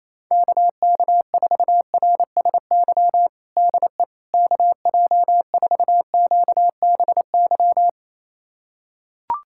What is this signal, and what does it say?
Morse code, heard as a keyed tone, about 28 wpm, 700 Hz. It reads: KK4RSY DE KJ4QBY
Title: USA Callsign DE Callsign - 6 Characters - 28wpm